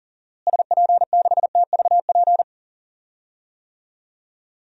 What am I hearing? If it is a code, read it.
Morse code, heard as a keyed tone, about 40 wpm, 700 Hz. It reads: SP6TVP
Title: DXCC-100 Call Signs - 6 Characters - Repeated 2x - 40wpm